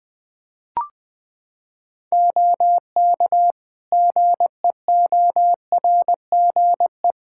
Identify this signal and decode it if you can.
Morse code, heard as a keyed tone, about 21 wpm, 700 Hz. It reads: OK GEORGE
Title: K1USN SST QSO Practice 20wpm